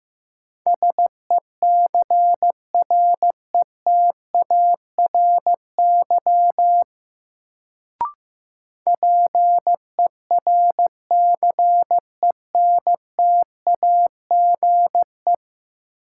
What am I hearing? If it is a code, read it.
Morse code, heard as a keyed tone, about 16 wpm, 700 Hz. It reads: SECRETARY PERCENTAGE
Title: Sentences from Top 1300 Words - New Word Drill - 15wpm